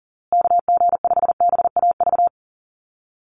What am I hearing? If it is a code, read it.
Morse code, heard as a keyed tone, about 39 wpm, 700 Hz. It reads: KG5BAV
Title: USA Call Signs Repeated 2x 40wpm